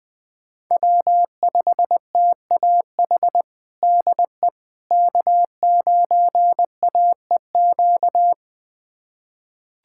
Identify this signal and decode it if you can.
Morse code, heard as a keyed tone, about 20 wpm, 700 Hz. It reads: W5TAH DE K9AEQ
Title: USA Callsign DE Callsign - 5 Characters - 20wpm